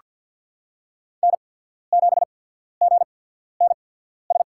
Morse code, heard as a keyed tone, about 47 wpm, spaced as for 13 wpm, 700 Hz. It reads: N7GNS
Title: USA Call Signs - 5 Characters - 8x Character Spacing - 50wpm